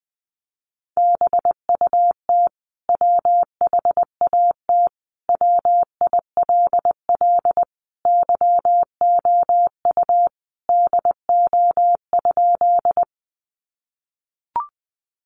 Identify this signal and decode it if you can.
Morse code, heard as a keyed tone, about 20 wpm, 700 Hz. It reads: BUT WHAT WILL YOU DO?
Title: Sentences from Top 100 Words 20wpm